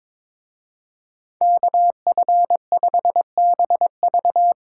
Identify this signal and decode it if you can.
Morse code, heard as a keyed tone, about 25 wpm, 700 Hz. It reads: KF5BV
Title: Speed Racing - USA 2x2 Call Signs 22wpm